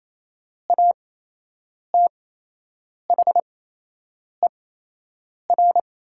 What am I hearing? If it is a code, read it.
Morse code, heard as a keyed tone, about 27 wpm, spaced as for 8 wpm, 700 Hz. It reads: ATHER
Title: 8x Character Spacing - Top 200 Words - 28wpm